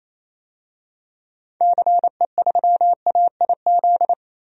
Morse code, heard as a keyed tone, about 28 wpm, 700 Hz. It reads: CE3AIZ
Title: DXCC-100 Call Signs - 6 Characters - 28wpm